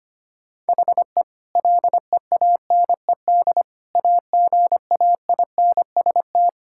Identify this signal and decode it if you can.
Morse code, heard as a keyed tone, about 25 wpm, 700 Hz. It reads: HE LEANED AGAINST
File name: practice / 25wpm / Sentences with 2-Word Phrases - 25wpm